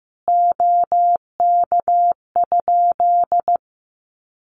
Morse code, heard as a keyed tone, about 15 wpm, 700 Hz. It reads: OK?
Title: Common QSO Phrases 15wpm